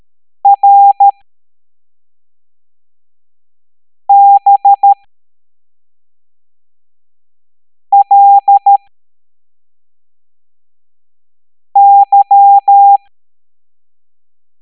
Morse code, heard as a keyed tone, about 13 wpm, spaced as for 3 wpm, 800 Hz. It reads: RBLY